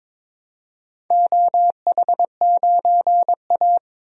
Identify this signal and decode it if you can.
Morse code, heard as a keyed tone, about 22 wpm, 700 Hz. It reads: OH9A
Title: DXCC-100 Call Signs - 4 Characters - Repeated 2x - 22wpm